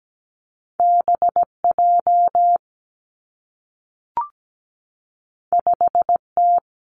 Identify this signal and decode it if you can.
Morse code, heard as a keyed tone, about 17 wpm, 700 Hz. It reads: B J 5 T
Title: Two Challenging Letter-Number Combinations 17wpm